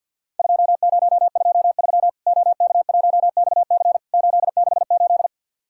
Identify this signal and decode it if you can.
Morse code, heard as a keyed tone, about 50 wpm, 700 Hz. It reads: 2023 OK1XC 868
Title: Worked All Europe DX Contest - QTC Exchange with Repeats - 50wpm